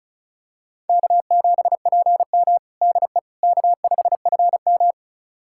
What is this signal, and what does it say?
Morse code, heard as a keyed tone, about 36 wpm, 700 Hz. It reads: K7PM DE K5FM
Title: USA Callsign DE Callsign - 4 Characters - 35wpm